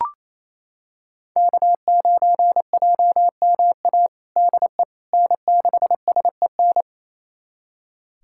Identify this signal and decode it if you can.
Morse code, heard as a keyed tone, about 28 wpm, 700 Hz. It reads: K9JMA DE N6SEN
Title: USA Callsign DE Callsign - 5 Characters - 28wpm